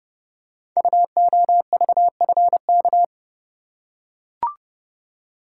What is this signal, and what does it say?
Morse code, heard as a keyed tone, about 28 wpm, 700 Hz. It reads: UOVFK
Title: Code Groups - 5 Letters - 30wpm